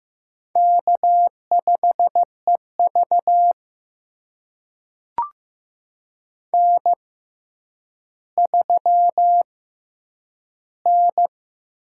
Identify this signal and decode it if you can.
Morse code, heard as a keyed tone, about 15 wpm, spaced as for 9 wpm, 700 Hz. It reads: K5EV N3N
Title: USA Call Signs - 4 Characters - 6x Character Spacing - 15wpm